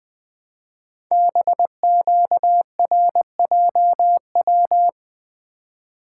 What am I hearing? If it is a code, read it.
Morse code, heard as a keyed tone, about 20 wpm, 700 Hz. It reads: BQRJW
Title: Code Groups - 5 Letters - 20wpm